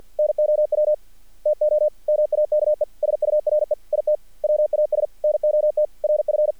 Morse code, heard as a keyed tone, about 48 wpm, 610 Hz. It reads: NOW TO MAKE SURE IT WAS NOT RF
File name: archived / CRSnet